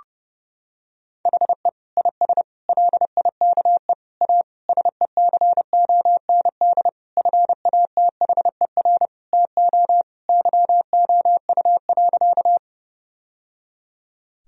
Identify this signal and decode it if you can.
Morse code, heard as a keyed tone, about 32 wpm, 700 Hz. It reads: HE IS LIKE A SECOND FATHER TO YOU.
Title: Sentences from Top 1000 Words - Review Part 1 - 30wpm